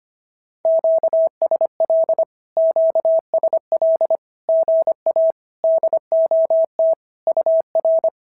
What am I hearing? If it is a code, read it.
Morse code, heard as a keyed tone, about 25 wpm, 650 Hz. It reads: QSL QSL GA DOT UR
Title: SOTA QSO Practice 25wpm